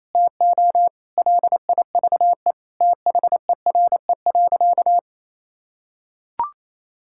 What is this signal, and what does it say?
Morse code, heard as a keyed tone, about 28 wpm, 700 Hz. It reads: TO LIVE THERE.
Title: Sentences from Top 1000 Words - Review Part 6 - 28wpm